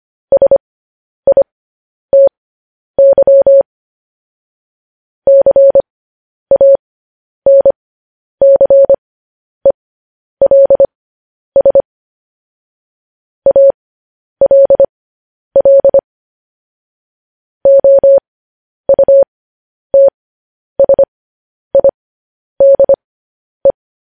Morse code, heard as a keyed tone, about 25 wpm, spaced as for 10 wpm, 550 Hz. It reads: SITY CANCELS ALL OUTSIDE